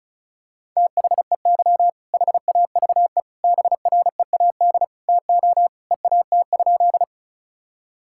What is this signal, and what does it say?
Morse code, heard as a keyed tone, about 36 wpm, 700 Hz. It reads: THEY HAVE BREAD TO EAT?
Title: Sentences from Top 1600 Words 35wpm